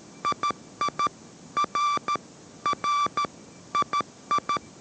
Morse code, heard as a keyed tone, about 13 wpm, 1170 Hz. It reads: II R R II